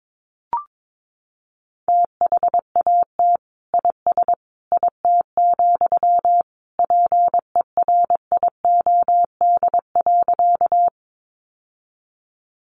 Morse code, heard as a keyed tone, about 22 wpm, 700 Hz. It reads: THAT IS IT, PERIOD.
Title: Sentences from Top 800 Words 22wpm